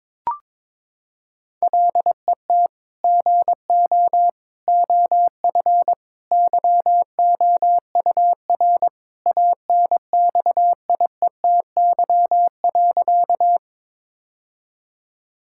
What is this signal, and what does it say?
Morse code, heard as a keyed tone, about 22 wpm, 700 Hz. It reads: LET GO OF YOUR ANXIETY.
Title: Sentences from Top 1600 Words 22wpm